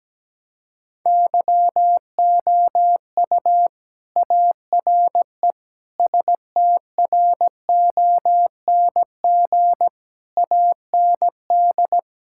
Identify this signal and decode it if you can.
Morse code, heard as a keyed tone, about 17 wpm, 700 Hz. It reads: YOU ARE STRONG AND
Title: Sentences from Top 1000 Words - Review Part 5 - 17wpm